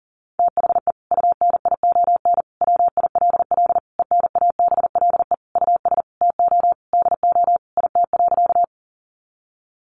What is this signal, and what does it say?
Morse code, heard as a keyed tone, about 37 wpm, 700 Hz. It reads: THE UNION WILL ENABLE US TO DO IT.
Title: Sentences from Top 1500 Words 40wpm